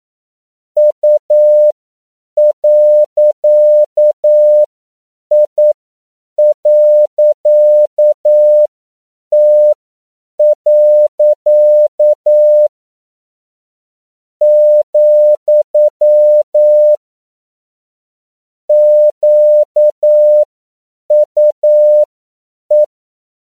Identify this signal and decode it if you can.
Morse code, heard as a keyed tone, about 9 wpm, spaced as for 7 wpm, 600 Hz. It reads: U.I.T. , QUE
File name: CW / 10PPM